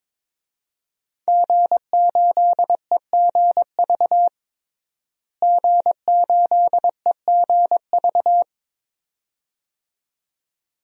Morse code, heard as a keyed tone, about 22 wpm, 700 Hz. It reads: G8EGV G8EGV
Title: DXCC-100 Call Signs - 5 Characters - Repeated 2x - 22wpm